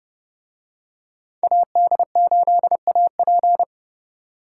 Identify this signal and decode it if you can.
Morse code, heard as a keyed tone, about 28 wpm, 700 Hz. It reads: AD8AP